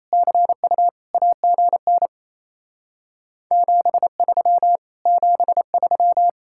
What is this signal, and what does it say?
Morse code, heard as a keyed tone, about 37 wpm, 700 Hz. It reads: CU AGN 73 73